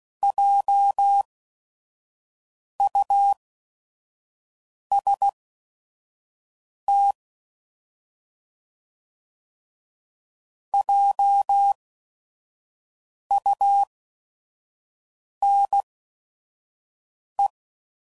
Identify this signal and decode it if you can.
Morse code, heard as a keyed tone, about 16 wpm, spaced as for 5 wpm, 800 Hz. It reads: JUST JUNE